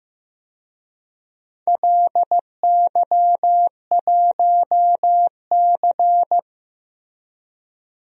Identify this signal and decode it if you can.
Morse code, heard as a keyed tone, about 15 wpm, 700 Hz. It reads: LY1C
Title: DXCC-100 Call Signs - 4 Characters - Repeated 3x - 15wpm